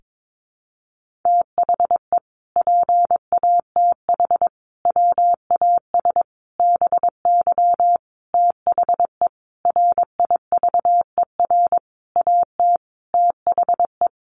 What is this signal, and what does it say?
Morse code, heard as a keyed tone, about 22 wpm, 700 Hz. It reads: THE PATH WAS BY THE RIVER AT THE
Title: Sentences from Top 1400 Words 22wpm